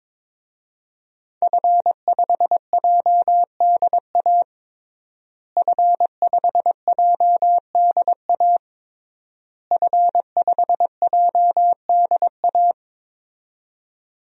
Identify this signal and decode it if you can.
Morse code, heard as a keyed tone, about 22 wpm, 700 Hz. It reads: F5JDA F5JDA F5JDA
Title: DXCC-100 Call Signs - 5 Characters - Repeated 3x - 22wpm